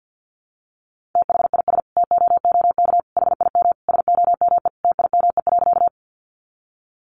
Morse code, heard as a keyed tone, about 50 wpm, 700 Hz. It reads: THIS TOOK HIM SOME TIME.
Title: Sentences from Top 300 Words 50wpm